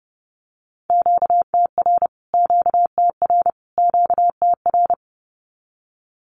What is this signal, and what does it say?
Morse code, heard as a keyed tone, about 30 wpm, 700 Hz. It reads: QTR QTR QTR
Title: Q-Codes Repeated 3x 30wpm